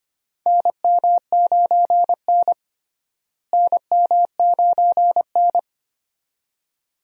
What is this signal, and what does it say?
Morse code, heard as a keyed tone, about 25 wpm, 700 Hz. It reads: NM9N NM9N
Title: USA Call Signs - 4 Characters - Repeated 2x - 25wpm